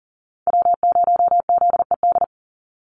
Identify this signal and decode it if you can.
Morse code, heard as a keyed tone, about 40 wpm, 700 Hz. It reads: W0ZED
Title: USA Call Signs 40wpm